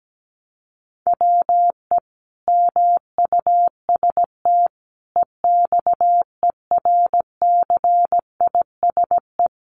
Morse code, heard as a keyed tone, about 17 wpm, 700 Hz. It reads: WE MUST EXERCISE
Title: Sentences from Top 1200 Words 17wpm